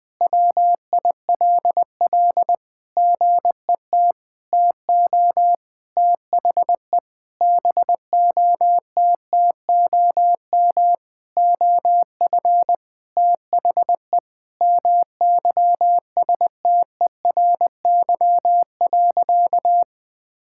Morse code, heard as a keyed tone, about 20 wpm, 700 Hz. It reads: WILL GET TO THE BOTTOM OF THE MYSTERY.
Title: Sentences with 6-Word Phrases - 20wpm